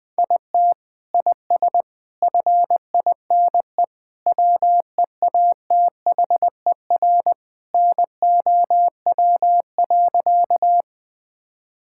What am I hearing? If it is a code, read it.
Morse code, heard as a keyed tone, about 20 wpm, 700 Hz. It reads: IT IS FINE WEATHER NOW.
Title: Sentences from Top 1000 Words - Review Part 5 - 20wpm